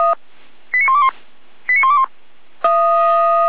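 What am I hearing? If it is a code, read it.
Five-tone selective call (ZVEI), heard as quick runs of five tones, tones of about 70 ms. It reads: 89121 89121
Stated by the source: ZVEI five-tone selective call 89119 received by SDR.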